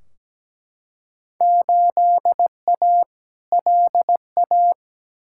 Morse code, heard as a keyed tone, about 17 wpm, 700 Hz. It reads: 8A LA
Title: Field Day Exchanges Encore - 17wpm